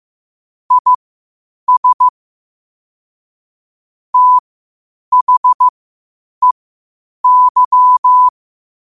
Morse code, heard as a keyed tone, about 15 wpm, spaced as for 8 wpm, 1000 Hz. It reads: IS THEY
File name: qotdfiles / 8wpm